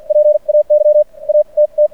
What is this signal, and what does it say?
Morse code, heard as a keyed tone, about 50 wpm, spaced as for 39 wpm, 590 Hz. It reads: PAY AT T
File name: archived / CRSnet